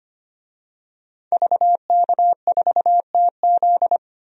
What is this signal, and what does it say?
Morse code, heard as a keyed tone, about 25 wpm, 700 Hz. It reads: VK4TZ
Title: DXCC-100 Call Signs - 5 Characters - 25wpm